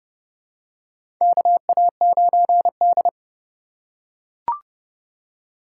Morse code, heard as a keyed tone, about 30 wpm, 700 Hz. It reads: KA9D
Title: USA Call Signs - 4 Characters - 2x Character Spacing - 30wpm